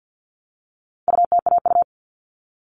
Morse code, heard as a keyed tone, about 50 wpm, 700 Hz. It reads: UTAU
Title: All Cut Numbers - 4-Digit 50wpm